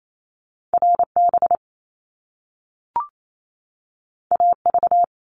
Morse code, heard as a keyed tone, about 28 wpm, 700 Hz. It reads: RB AV
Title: Top Two Letter Combinations in Words 28wpm